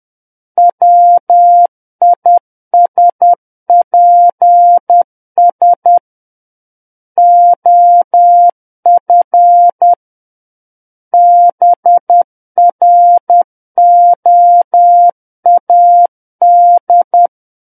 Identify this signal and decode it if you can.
Morse code, heard as a keyed tone, about 10 wpm, 700 Hz. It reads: WISPS OF BROAD
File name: Book One / 01-Chapter 1 - 10 WPM